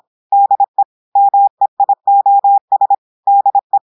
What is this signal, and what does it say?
Morse code, heard as a keyed tone, about 26 wpm, 800 Hz. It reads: DE MEIOS DE
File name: portugais texte en clair 2 vitesse 26 mots minutes